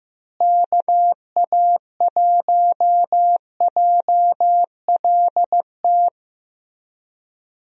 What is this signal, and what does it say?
Morse code, heard as a keyed tone, about 15 wpm, 700 Hz. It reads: KA1JLT
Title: USA Call Signs - 6 Characters - Repeated 2x - 15wpm